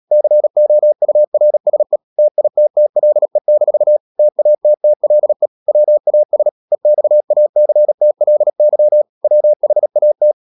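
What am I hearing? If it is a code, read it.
Morse code, heard as a keyed tone, about 37 wpm, 600 Hz. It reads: COURSE TITTLE-TATTLE WAS EXACTLY WHAT
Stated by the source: Morse code practice at 35–39 wpm sending English text of Ken Follett's novel Never, from CW med Gnister.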